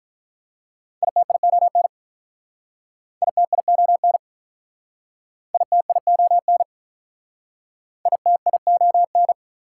Morse code, heard as a keyed tone, about 45 wpm, 700 Hz. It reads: ITION ITION ITION ITION